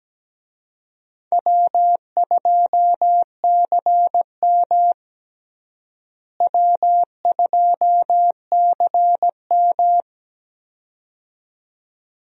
Morse code, heard as a keyed tone, about 17 wpm, 700 Hz. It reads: W2CM W2CM
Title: USA Call Signs - 4 Characters - Repeated 2x - 17wpm